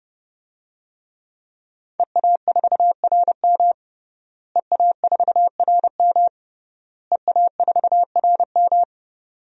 Morse code, heard as a keyed tone, about 29 wpm, 700 Hz. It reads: EA4RM EA4RM EA4RM
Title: DXCC-100 Call Signs - 5 Characters - Repeated 3x - 30wpm